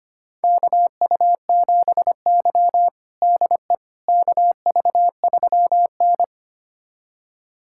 Morse code, heard as a keyed tone, about 25 wpm, 700 Hz. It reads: KU7Y DE KV3N
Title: USA Callsign DE Callsign - 4 Characters - 25wpm